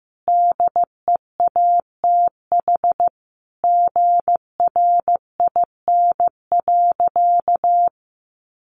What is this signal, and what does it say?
Morse code, heard as a keyed tone, about 15 wpm, 700 Hz. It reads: DEATH GRIN.
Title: A Sound Of Thunder 15wpm